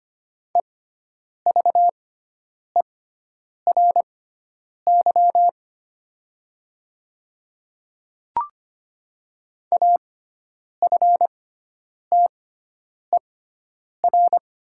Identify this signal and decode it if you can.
Morse code, heard as a keyed tone, about 25 wpm, spaced as for 9 wpm, 700 Hz. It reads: EVERY AFTER
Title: 6x Character Spacing - Top 200 Words - 25wpm